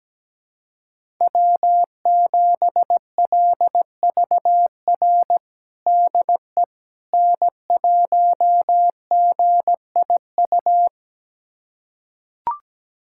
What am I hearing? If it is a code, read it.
Morse code, heard as a keyed tone, about 17 wpm, 700 Hz. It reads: W7LVR DE N1GIU
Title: USA Callsign DE Callsign - 5 Characters - 17wpm